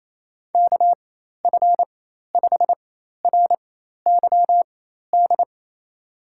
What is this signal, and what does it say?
Morse code, heard as a keyed tone, about 28 wpm, spaced as for 13 wpm, 700 Hz. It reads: KF5RYD
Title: USA Call Signs - 6 Characters - 4x Character Spacing - 28wpm